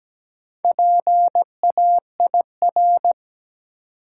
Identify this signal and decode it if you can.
Morse code, heard as a keyed tone, about 17 wpm, 700 Hz. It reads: PAIR
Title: Four Letter Words - 17wpm